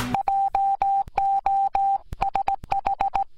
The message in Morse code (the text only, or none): JOSH